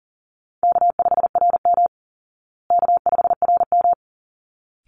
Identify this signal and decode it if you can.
Morse code, heard as a keyed tone, about 37 wpm, 700 Hz. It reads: K5RM K5RM
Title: Easy Call Signs Repeated 2x - 40wpm